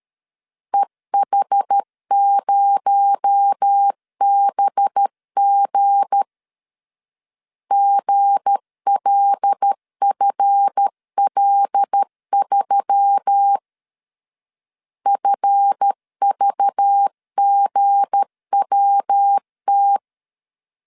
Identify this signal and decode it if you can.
Morse code, heard as a keyed tone, about 13 wpm, 800 Hz. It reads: EH0BG GLFL3 FVGWT
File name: LEZIONI / 7 GF9Z / 7 GF 12wpm 2122